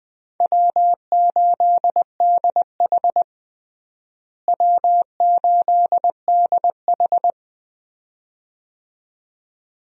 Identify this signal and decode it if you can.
Morse code, heard as a keyed tone, about 21 wpm, 700 Hz. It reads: W8DH W8DH
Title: USA Call Signs - 4 Characters - Repeated 2x - 20wpm